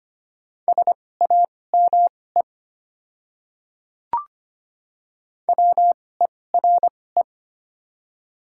Morse code, heard as a keyed tone, about 25 wpm, spaced as for 18 wpm, 700 Hz. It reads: SAME WERE
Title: 2x Character Spacing - Top 500 Words - 25wpm